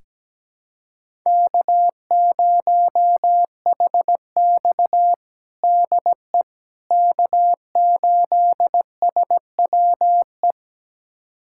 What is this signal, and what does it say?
Morse code, heard as a keyed tone, about 17 wpm, 700 Hz. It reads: K0HX DE K8SWE
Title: USA Callsign DE Callsign 17wpm